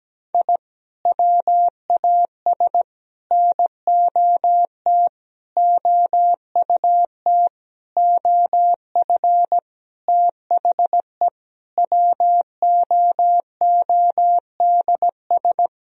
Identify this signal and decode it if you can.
Morse code, heard as a keyed tone, about 17 wpm, 700 Hz. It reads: I WAS NOT OUT OF THE WOODS
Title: Sentences from Top 1000 Words - Review Part 2 - 17wpm